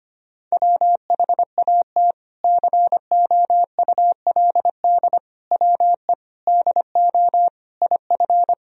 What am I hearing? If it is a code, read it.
Morse code, heard as a keyed tone, about 25 wpm, 700 Hz. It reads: WHAT COULD WE DO IF
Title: Sentences from Top 400 Words and Abbreviations 25wpm